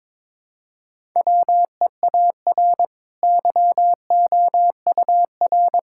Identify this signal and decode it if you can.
Morse code, heard as a keyed tone, about 22 wpm, 700 Hz. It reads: WEAR YOUR
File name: practice / 22wpm / Easy Sentences with Hints - 22wpm